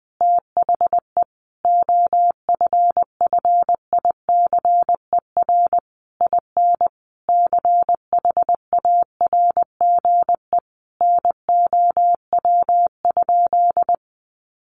Morse code, heard as a keyed tone, about 20 wpm, 700 Hz. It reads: THE OFFICER IN CHARGE NOW?
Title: Sentences from Top 1200 Words 20wpm